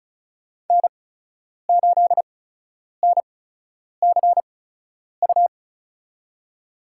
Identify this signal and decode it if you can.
Morse code, heard as a keyed tone, about 36 wpm, spaced as for 10 wpm, 700 Hz. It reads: N8NCU
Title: USA Call Signs - 5 Characters - 8x Character Spacing - 35wpm